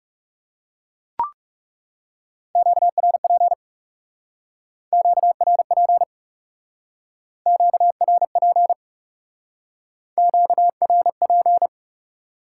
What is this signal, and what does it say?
Morse code, heard as a keyed tone, about 40 wpm, 700 Hz. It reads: QRP QRP QRP QRP